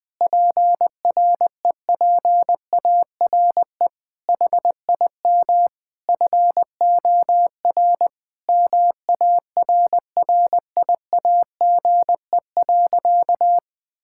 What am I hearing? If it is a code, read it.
Morse code, heard as a keyed tone, about 20 wpm, 700 Hz. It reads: PREPARE HIM FOR MARRIAGE.
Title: Sentences from Top 1000 Words - Review Part 6 - 20wpm